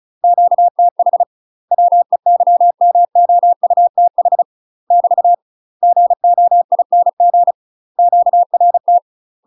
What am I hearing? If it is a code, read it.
Morse code, heard as a keyed tone, about 36 wpm, 700 Hz. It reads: QTH WEYMOUTH <BT> GOING QRT
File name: UK MASTER QSOs TRAINING / 35WPM / UK MASTER QSOs 0041